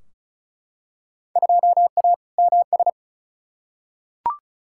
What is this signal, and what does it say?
Morse code, heard as a keyed tone, about 36 wpm, 700 Hz. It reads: 2A MS
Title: Field Day Exchanges - 35wpm